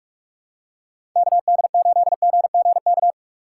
Morse code, heard as a keyed tone, about 44 wpm, 700 Hz. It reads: KD8GGK
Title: USA Call Signs - 6 Characters - 6x Character Spacing - 45wpm